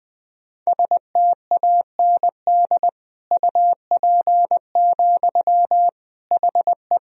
Morse code, heard as a keyed tone, about 20 wpm, 700 Hz. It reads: STAND UP, HE
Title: Sentences from Top 1000 Words - Review Part 1 - 20wpm